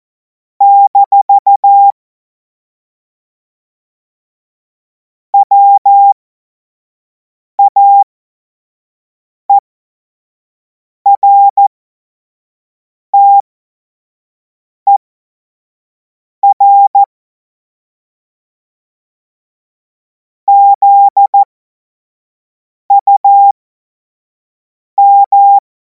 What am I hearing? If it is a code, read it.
Morse code, heard as a keyed tone, about 14 wpm, spaced as for 5 wpm, 800 Hz. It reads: - WAERTER ZUM
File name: Morsekurs / Witze / Witz 31. März 2026 25 F70